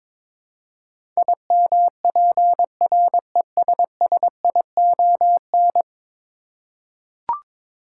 Morse code, heard as a keyed tone, about 22 wpm, 700 Hz. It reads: IMPRESSION
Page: morse-code-ninja-practice-22wpm